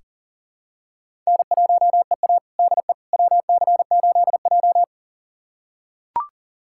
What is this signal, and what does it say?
Morse code, heard as a keyed tone, about 40 wpm, 700 Hz. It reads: N1EA DE WC8J
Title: USA Callsign DE Callsign - 4 Characters - 40wpm